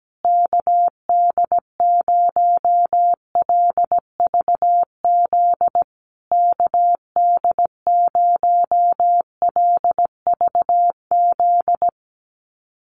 Morse code, heard as a keyed tone, about 17 wpm, 700 Hz. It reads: KD0LVZ KD0LVZ
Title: USA Call Signs Repeated 3x 17wpm